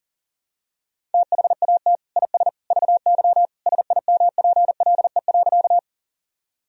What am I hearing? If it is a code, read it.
Morse code, heard as a keyed tone, about 37 wpm, 700 Hz. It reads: THAT IS VY SIMPLE.
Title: Sentences from Top 400 Words and Abbreviations 40wpm